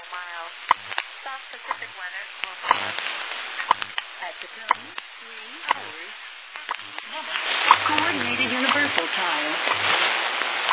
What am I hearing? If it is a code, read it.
Radio time signal, with speech.